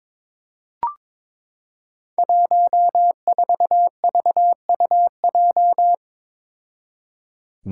Morse code, heard as a keyed tone, about 22 wpm, 700 Hz. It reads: 14VUJ
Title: Five Challenging Letter-Number Combinations 22wpm